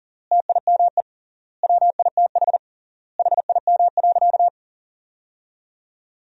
Morse code, heard as a keyed tone, about 40 wpm, 700 Hz. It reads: TIME WITH HIM.
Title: 3x Word-Spacing - Sentences from Top 300 Words - 40wpm